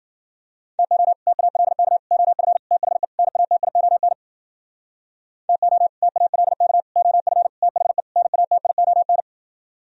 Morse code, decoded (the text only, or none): TO TALK OF THE NATION TO TALK OF THE NATION